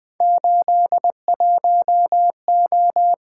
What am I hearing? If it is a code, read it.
Morse code, heard as a keyed tone, about 21 wpm, 700 Hz. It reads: 81O
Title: Three Letter-Number Combinations - 20wpm